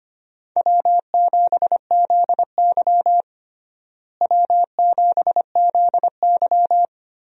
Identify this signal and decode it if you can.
Morse code, heard as a keyed tone, about 25 wpm, 700 Hz. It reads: W7ZY W7ZY
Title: USA Call Signs - 4 Characters - Repeated 3x - 25wpm